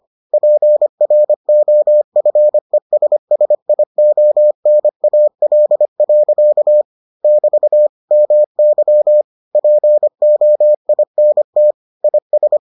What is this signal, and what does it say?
Morse code, heard as a keyed tone, about 25 wpm, 600 Hz. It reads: PROFESSIONAL. <BT> MY POINT IS